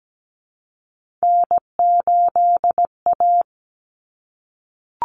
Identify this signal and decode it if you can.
Morse code, heard as a keyed tone, about 17 wpm, 700 Hz. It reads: N8A
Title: Common Cut Numbers - 3-Digit 17wpm